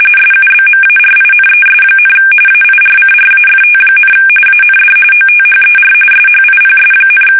RTTY, about 45.45 baud. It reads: SUBMARINES, AIRCRAFT, AND OTHER SHIPS,
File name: fsk850 / battleshipnj